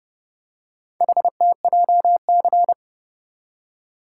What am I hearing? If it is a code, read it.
Morse code, heard as a keyed tone, about 29 wpm, 700 Hz. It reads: HTJC